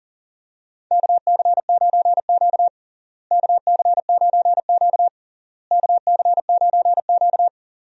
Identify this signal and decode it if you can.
Morse code, heard as a keyed tone, about 40 wpm, 700 Hz. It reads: KC9Q KC9Q KC9Q
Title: USA Call Signs - 4 Characters - Repeated 3x - 40wpm